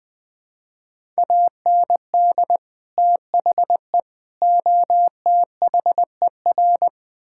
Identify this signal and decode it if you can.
Morse code, heard as a keyed tone, about 20 wpm, 700 Hz. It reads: AND THE OTHER
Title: Build A Phrase - 3-Word Phrases - 20wpm